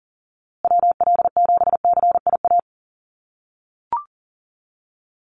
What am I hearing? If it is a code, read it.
Morse code, heard as a keyed tone, about 40 wpm, 700 Hz. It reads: WL7CIA
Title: USA Call Signs Repeated 2x 40wpm